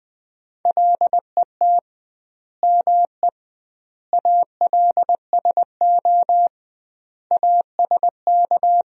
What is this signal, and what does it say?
Morse code, heard as a keyed tone, about 20 wpm, 700 Hz. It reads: LET ME ALSO ASK
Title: 2x Word-Spacing - Sentences from Top 400 Words - 20wpm